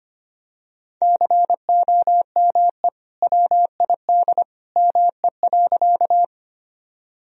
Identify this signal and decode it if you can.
Morse code, heard as a keyed tone, about 25 wpm, 700 Hz. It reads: COME WID ME.
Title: Sentences from Top 200 Words and Abbreviations 25wpm